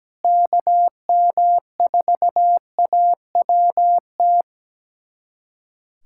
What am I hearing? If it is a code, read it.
Morse code, heard as a keyed tone, about 17 wpm, 700 Hz. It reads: KM4AWT
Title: USA Callsign DE Callsign - 6 Characters - 17wpm